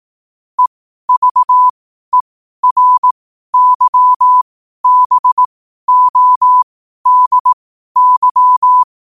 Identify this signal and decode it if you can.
Morse code, heard as a keyed tone, about 18 wpm, spaced as for 12 wpm, 1000 Hz. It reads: EVERYBODY
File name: qotdfiles / 12wpm